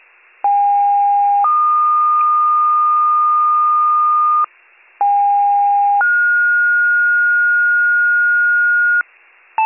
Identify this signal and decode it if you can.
Two-tone page, heard as a one-second tone, then a three-second tone, about 810 then 1220 Hz, about 810 then 1430 Hz, with noise. It is heard from 0.4 to 9.0 s.